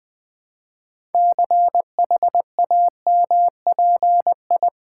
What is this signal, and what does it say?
Morse code, heard as a keyed tone, about 20 wpm, 700 Hz. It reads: CHAMPI
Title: Build A Word - 3 Letters At A Time - 8-Letter Words - 20wpm